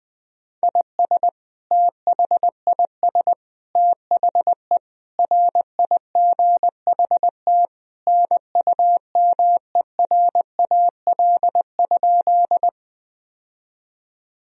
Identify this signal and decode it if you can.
Morse code, heard as a keyed tone, about 20 wpm, 700 Hz. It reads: IS THIS THE RIGHT NUMERAL?
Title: Sentences from Top 1000 Words - Review Part 2 - 20wpm